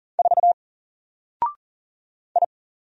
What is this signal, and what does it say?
Morse code, heard as a keyed tone, about 38 wpm, 700 Hz. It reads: 4 I
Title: Single Character (Letter:Number:Punctuation:Prosign) 40wpm